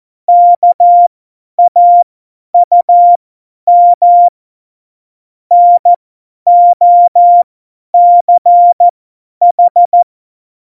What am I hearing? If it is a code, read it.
Morse code, heard as a keyed tone, about 14 wpm, spaced as for 10 wpm, 700 Hz. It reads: KAUM NOCH
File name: Morsekurs / Witze / Witz 05. Januar 2021, 50 F70